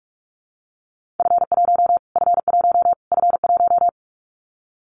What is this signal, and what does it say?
Morse code, heard as a keyed tone, about 44 wpm, 700 Hz. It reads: F1 F1 F1
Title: Callsign Prefixes for Top 25 DXCC - Repeated 3x - 45wpm